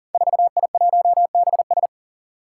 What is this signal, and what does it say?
Morse code, heard as a keyed tone, about 40 wpm, 700 Hz. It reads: 4I1BS
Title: Five Challenging Letter-Number Combinations 40wpm